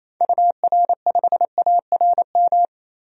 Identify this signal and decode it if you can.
Morse code, heard as a keyed tone, about 28 wpm, 700 Hz. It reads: UR5ARM
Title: DXCC-100 Call Signs - 6 Characters - 28wpm